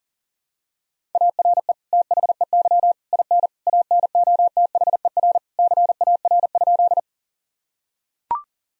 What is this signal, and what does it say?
Morse code, heard as a keyed tone, about 40 wpm, 700 Hz. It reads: ARE THEY IN ANOTHER CAR?
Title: Sentences from Top 1000 Words - Review Part 1 - 40wpm